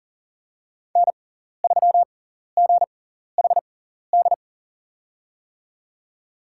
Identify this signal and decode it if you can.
Morse code, heard as a keyed tone, about 40 wpm, spaced as for 14 wpm, 700 Hz. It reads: N3GHD
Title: USA Call Signs - 5 Characters - 6x Character Spacing - 40wpm